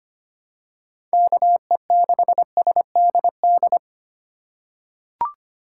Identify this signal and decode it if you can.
Morse code, heard as a keyed tone, about 25 wpm, 700 Hz. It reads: KE6SDD
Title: USA Call Signs - 3x Character Spacing - 25wpm